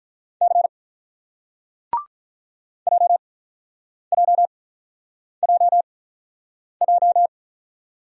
Morse code, heard as a keyed tone, about 48 wpm, spaced as for 9 wpm, 700 Hz. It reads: C JJJJ